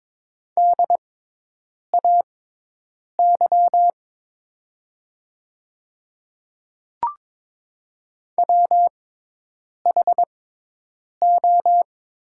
Morse code, heard as a keyed tone, about 22 wpm, spaced as for 8 wpm, 700 Hz. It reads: DAY WHO
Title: 6x Character Spacing - Top 200 Words - 22wpm